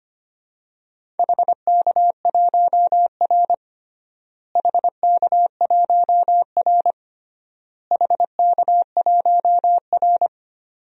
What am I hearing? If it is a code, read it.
Morse code, heard as a keyed tone, about 25 wpm, 700 Hz. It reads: HK1R HK1R HK1R
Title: DXCC-100 Call Signs - 4 Characters - Repeated 3x - 25wpm